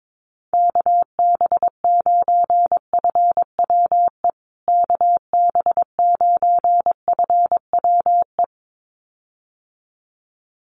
Morse code, heard as a keyed tone, about 22 wpm, 700 Hz. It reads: KB9FWE KB9FWE
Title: USA Call Signs Repeated 2x 22wpm